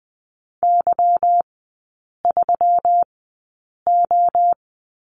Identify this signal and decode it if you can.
Morse code, heard as a keyed tone, about 19 wpm, spaced as for 8 wpm, 690 Hz. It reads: Y3O
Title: 212 Lesson 34e - Number 8 - Three Characters 20wpm